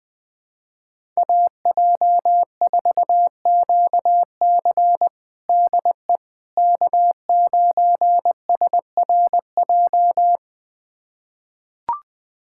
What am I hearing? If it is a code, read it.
Morse code, heard as a keyed tone, about 20 wpm, 700 Hz. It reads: AJ4QC DE K9SRJ
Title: USA Callsign DE Callsign - 5 Characters - 20wpm